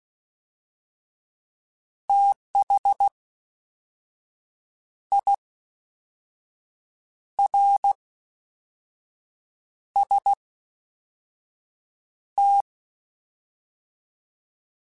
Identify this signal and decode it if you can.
Morse code, heard as a keyed tone, about 16 wpm, 800 Hz. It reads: TH I R S T